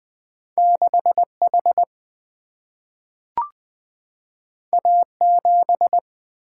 Morse code, heard as a keyed tone, about 20 wpm, 700 Hz. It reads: 6H A7
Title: Single-Letter Single-Number 20wpm